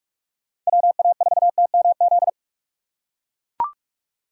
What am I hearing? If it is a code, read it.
Morse code, heard as a keyed tone, about 43 wpm, 700 Hz. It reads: WA4TMZ
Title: USA Call Signs - 6 Characters - 6x Character Spacing - 45wpm